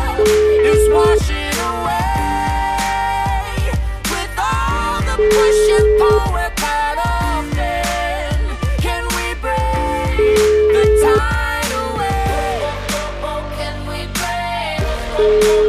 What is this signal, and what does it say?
Ringback tone (Europe, 425 Hz), with noise.